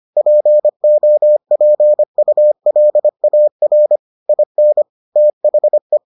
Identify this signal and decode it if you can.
Morse code, heard as a keyed tone, about 25 wpm, 600 Hz. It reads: POPULAR IN THE